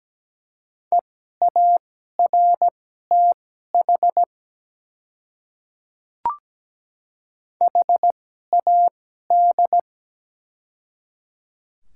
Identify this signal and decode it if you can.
Morse code, heard as a keyed tone, about 17 wpm, spaced as for 12 wpm, 700 Hz. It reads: EARTH HAD
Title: 2x Character Spacing - Top 300 Words - 17wpm